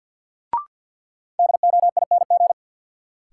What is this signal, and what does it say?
Morse code, heard as a keyed tone, about 53 wpm, 700 Hz. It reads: DOING
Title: Sets of 4 Words - New Word Drill - 50wpm